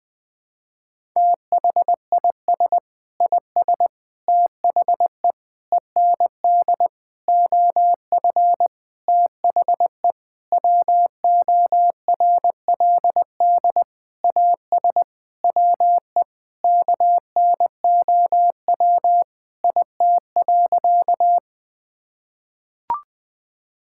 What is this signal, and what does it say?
Morse code, heard as a keyed tone, about 21 wpm, 700 Hz. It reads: THIS IS THE END OF THE WORLD AS WE KNOW IT.
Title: Sentences with 9-Word Phrases - Spoken Beginning - 20wpm